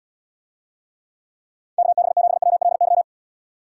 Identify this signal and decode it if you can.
Morse code, heard as a keyed tone, about 78 wpm, 700 Hz. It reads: KB7WLY